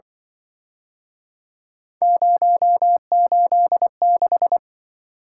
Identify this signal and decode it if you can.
Morse code, heard as a keyed tone, about 26 wpm, 700 Hz. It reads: 086